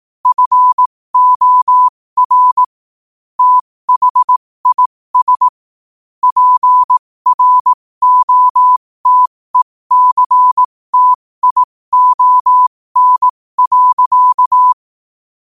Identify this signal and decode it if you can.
Morse code, heard as a keyed tone, about 18 wpm, spaced as for 15 wpm, 1000 Hz. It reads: FOR THIS PROTECTION.
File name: qotdfiles / 15wpm